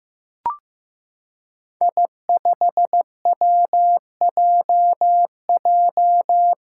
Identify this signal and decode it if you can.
Morse code, heard as a keyed tone, about 15 wpm, 700 Hz. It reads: I5WJJ
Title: Five Challenging Letter-Number Combinations 15wpm